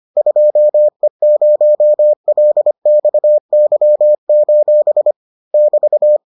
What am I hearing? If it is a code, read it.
Morse code, heard as a keyed tone, about 25 wpm, 600 Hz. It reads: 2E0LXY: <BT>